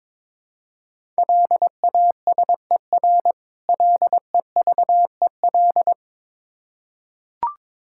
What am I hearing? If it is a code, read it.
Morse code, heard as a keyed tone, about 22 wpm, 690 Hz. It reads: LASER LEVEL
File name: Difficult Words with Hints - 22wpm